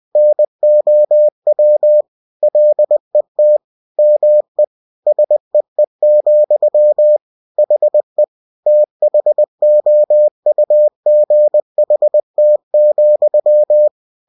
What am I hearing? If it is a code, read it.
Morse code, heard as a keyed tone, about 20 wpm, 600 Hz. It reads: NOW LET ME SEE, HE THOUGHT,